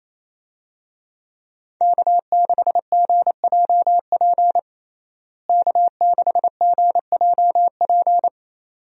Morse code, heard as a keyed tone, about 28 wpm, 700 Hz. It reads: K6GJP K6GJP
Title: USA Call Signs - 5 Characters - Repeated 2x - 28wpm